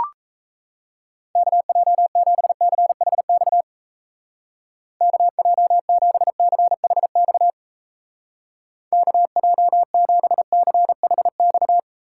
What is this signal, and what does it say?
Morse code, heard as a keyed tone, about 36 wpm, 700 Hz. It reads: KJ7CHX KJ7CHX KJ7CHX